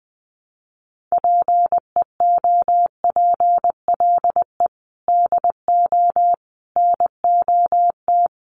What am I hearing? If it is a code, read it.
Morse code, heard as a keyed tone, about 19 wpm, 700 Hz. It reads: PEOPLE DO NOT
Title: Sets of 5 Words Encore2 20wpm